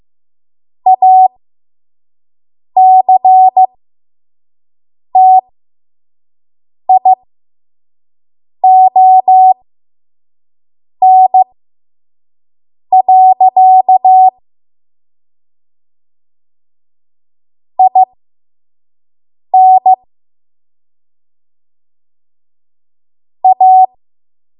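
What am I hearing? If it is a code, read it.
Morse code, heard as a keyed tone, about 15 wpm, spaced as for 5 wpm, 750 Hz. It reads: ACTION. IN A
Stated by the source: 5 WPM Morse code practice.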